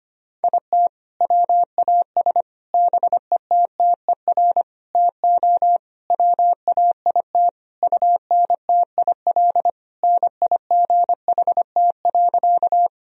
Morse code, heard as a keyed tone, about 25 wpm, 700 Hz. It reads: IT WAS BETTER TO WAIT UNTIL NIGHT.
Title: Sentences from Top 1000 Words - Review Part 1 - 25wpm